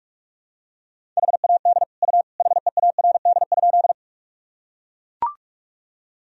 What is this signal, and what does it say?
Morse code, heard as a keyed tone, about 41 wpm, 700 Hz. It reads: HAD U HEARD?
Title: Sentences from Top 400 Words and Abbreviations 45wpm